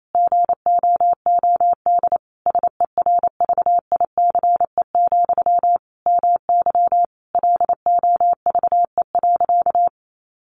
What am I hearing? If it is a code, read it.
Morse code, heard as a keyed tone, about 28 wpm, 700 Hz. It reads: GOOD SERVICE, MY LOVE.
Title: Sentences from Top 1100 Words 28wpm